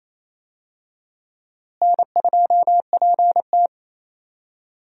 Morse code, heard as a keyed tone, about 28 wpm, 700 Hz. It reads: N2PT
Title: USA Call Signs - 4 Characters - Repeated 2x - 28wpm